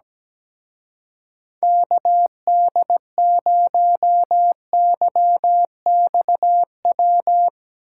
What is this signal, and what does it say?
Morse code, heard as a keyed tone, about 19 wpm, 700 Hz. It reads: KD0YXW
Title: Speed Racing - USA 2x3 Call Signs 17wpm